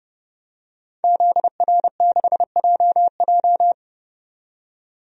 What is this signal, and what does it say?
Morse code, heard as a keyed tone, about 30 wpm, 700 Hz. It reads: ZR6JJ
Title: DXCC-100 Call Signs - 5 Characters - 30wpm